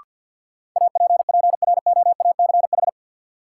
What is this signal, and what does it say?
Morse code, heard as a keyed tone, about 53 wpm, 700 Hz. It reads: APPROACH
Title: Sentences from Top 1000 Words - Review Part 6 - New Word Drill - 50wpm